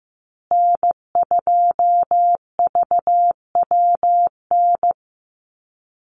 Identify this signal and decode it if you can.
Morse code, heard as a keyed tone, about 15 wpm, 700 Hz. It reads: N2VWN
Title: USA Call Signs Repeated 3x 15wpm